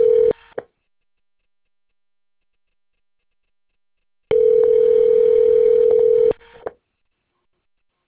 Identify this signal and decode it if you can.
Ringback tone (North America), 440 + 480 Hz.